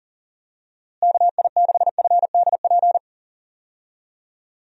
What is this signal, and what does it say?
Morse code, heard as a keyed tone, about 40 wpm, 700 Hz. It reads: KI6FDP
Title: USA Call Signs - 6 Characters - Repeated 3x - 40wpm